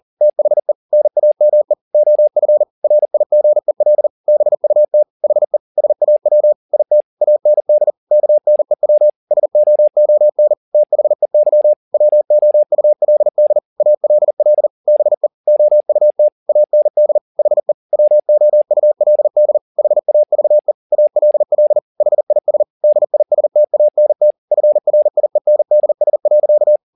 Morse code, heard as a keyed tone, about 40 wpm, 600 Hz. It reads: THE NAME OF RIGEL BUT HE SAW IT AND KNEW SOON THEY WOULD ALL BE OUT AND HE WOULD HAVE ALL HIS DISTANT FRIENDS.
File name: the old man and the sea 2 40wpm